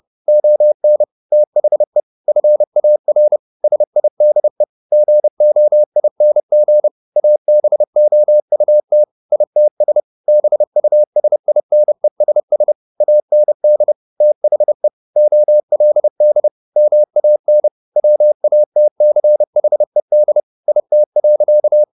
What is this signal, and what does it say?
Morse code, heard as a keyed tone, about 30 wpm, 600 Hz. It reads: ON THE FAR SIDE GOING ABOUT ITS BUSINESS AND THE OLD MAN WATCHED IT.